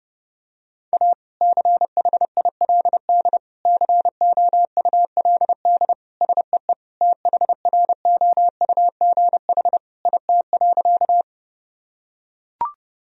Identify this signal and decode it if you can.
Morse code, heard as a keyed tone, about 30 wpm, 700 Hz. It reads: A CHILD COULD SEE THROUGH IT.
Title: Sentences from Top 1000 Words - Review Part 1 - 30wpm